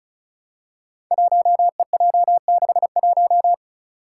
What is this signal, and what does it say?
Morse code, heard as a keyed tone, about 36 wpm, 700 Hz. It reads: 1EJ61
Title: Five Challenging Letter-Number Combinations 35wpm